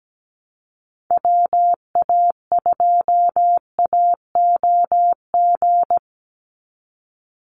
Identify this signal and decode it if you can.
Morse code, heard as a keyed tone, about 17 wpm, 700 Hz. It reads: WA2AOG
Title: Call Signs Repeated 2x - 17wpm